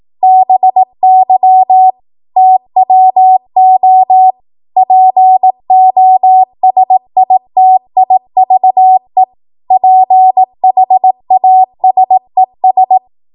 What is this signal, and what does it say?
Morse code, heard as a keyed tone, about 18 wpm, 750 Hz. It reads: BY TWO POSITIVE PHASES
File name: Archive / 18 WPM / 18WPM